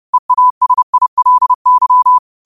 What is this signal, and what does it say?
Morse code, heard as a keyed tone, about 30 wpm, 1000 Hz. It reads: EASILY
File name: qotdfiles / 30wpm